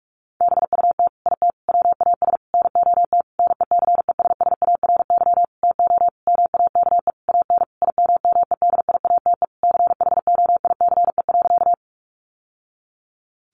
Morse code, heard as a keyed tone, about 44 wpm, 700 Hz. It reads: BUT IT WAS NOT NECESSARY TO MAKE AN IMMEDIATE CHOICE.
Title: Sentences from Top 1200 Words 45wpm